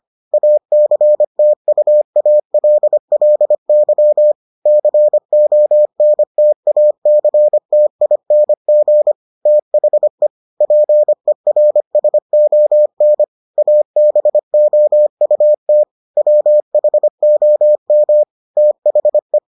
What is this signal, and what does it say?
Morse code, heard as a keyed tone, about 25 wpm, 600 Hz. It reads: ACTUALLY CONTACTING THE PERSON ABOUT WHOM THE